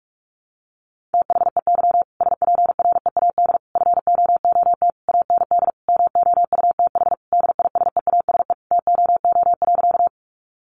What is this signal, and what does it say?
Morse code, heard as a keyed tone, about 44 wpm, 700 Hz. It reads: THEY SPREAD FOOT AND MOUTH DISEASE TOO.
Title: Sentences from Top 1500 Words 45wpm